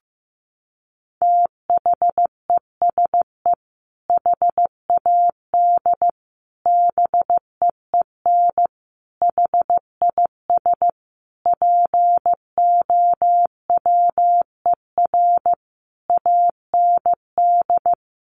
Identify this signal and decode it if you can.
Morse code, heard as a keyed tone, about 15 wpm, 700 Hz. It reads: THESE HAD BEEN HIS POWER AND
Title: Sentences from Top 1500 Words 15wpm